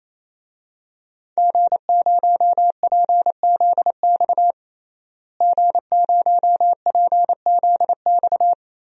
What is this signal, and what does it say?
Morse code, heard as a keyed tone, about 28 wpm, 700 Hz. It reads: G0PZX G0PZX
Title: DXCC-100 Call Signs - 5 Characters - Repeated 3x - 28wpm